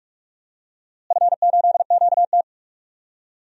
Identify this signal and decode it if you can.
Morse code, heard as a keyed tone, about 45 wpm, 700 Hz. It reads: F8QT
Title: DXCC-100 Call Signs - 4 Characters - Repeated 3x - 45wpm